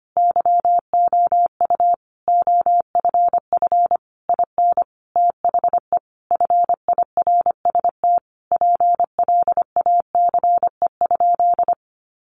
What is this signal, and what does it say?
Morse code, heard as a keyed tone, about 25 wpm, 700 Hz. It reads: YOU OFF IN THE FIRST PLACE?
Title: Sentences from Top 200 Words 25wpm